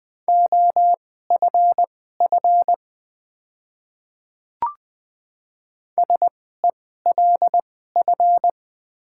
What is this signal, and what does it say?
Morse code, heard as a keyed tone, about 20 wpm, spaced as for 14 wpm, 700 Hz. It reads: OFF SELF
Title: 2x Character Spacing - Top 300 Words - 20wpm